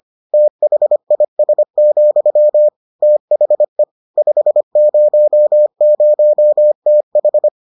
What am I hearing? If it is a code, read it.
Morse code, heard as a keyed tone, about 25 wpm, 600 Hz. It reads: THIS, THE 500TH